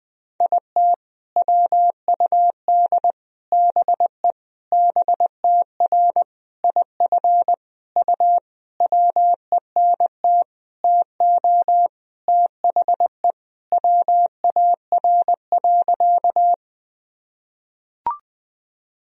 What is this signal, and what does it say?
Morse code, heard as a keyed tone, about 21 wpm, 700 Hz. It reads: IT WUD BE BTR IF U WENT TO THE WAR.
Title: Sentences from Top 400 Words and Abbreviations 20wpm